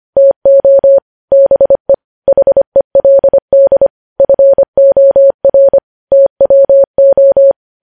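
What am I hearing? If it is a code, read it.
Morse code, heard as a keyed tone, about 25 wpm, 550 Hz. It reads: TO BE HELD FOR TWO
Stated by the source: Morse code news headlines sent at 25 words per minute.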